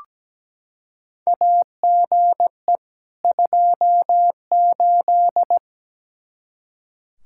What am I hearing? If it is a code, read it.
Morse code, heard as a keyed tone, about 17 wpm, 700 Hz. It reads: AGE 28
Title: Practice QSO - Exchange 3 - More Info - 17wpm